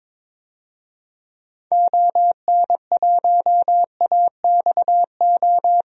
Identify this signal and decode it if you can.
Morse code, heard as a keyed tone, about 21 wpm, 700 Hz. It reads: ON1AXO
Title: DXCC-100 Call Signs - 6 Characters - 22wpm